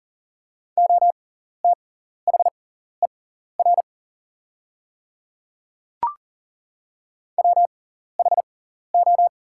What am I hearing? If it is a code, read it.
Morse code, heard as a keyed tone, about 40 wpm, spaced as for 14 wpm, 690 Hz. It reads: OTHER WHO
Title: 6x Character Spacing - Top 200 Words - 40wpm